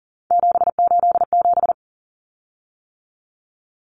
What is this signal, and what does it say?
Morse code, heard as a keyed tone, about 40 wpm, 700 Hz. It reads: 787
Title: Common Cut Numbers - 3-Digit 40wpm